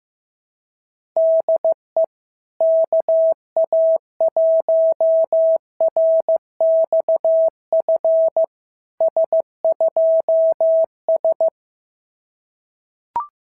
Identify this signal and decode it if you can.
Morse code, heard as a keyed tone, about 14 wpm, 650 Hz. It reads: DE KA1RXF S2S
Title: SOTA QSO Practice 15wpm